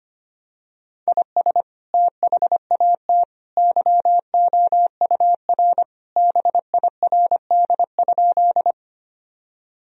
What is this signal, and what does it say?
Morse code, heard as a keyed tone, about 25 wpm, 700 Hz. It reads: IS THAT YOUR BIRD?
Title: Sentences from Top 1000 Words - Review Part 2 - 25wpm